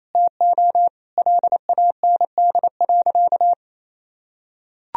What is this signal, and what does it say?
Morse code, heard as a keyed tone, about 28 wpm, 700 Hz. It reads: TO LAND.
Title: Sentences from Top 1000 Words - Review Part 2 - 28wpm